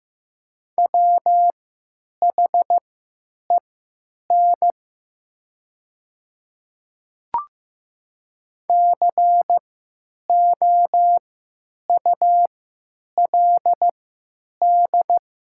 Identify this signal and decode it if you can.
Morse code, heard as a keyed tone, about 15 wpm, spaced as for 8 wpm, 700 Hz. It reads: WHEN COULD
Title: 3x Character Spacing - Top 100 Words - 15wpm